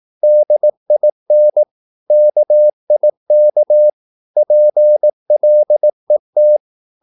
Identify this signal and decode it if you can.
Morse code, heard as a keyed tone, about 18 wpm, 600 Hz. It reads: DIN KIK PLET
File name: Korte ord DK 18wpm